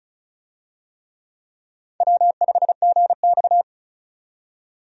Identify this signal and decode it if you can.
Morse code, heard as a keyed tone, about 35 wpm, 700 Hz. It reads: W5GX
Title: USA Call Signs - 4 Characters - 35wpm